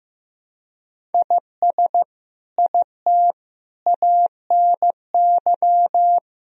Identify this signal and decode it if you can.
Morse code, heard as a keyed tone, about 15 wpm, 700 Hz. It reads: IS IT ANY
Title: Sentences from Top 200 Words and Abbreviations 15wpm